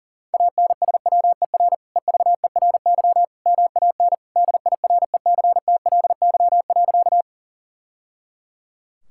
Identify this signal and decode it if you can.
Morse code, heard as a keyed tone, about 40 wpm, 700 Hz. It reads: ANSWER EVERY MAN DIRECTLY.
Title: Sentences from Top 1000 Words - Review Part 5 - 40wpm